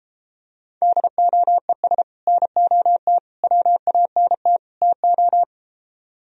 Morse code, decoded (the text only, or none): DOES NOT WANT TO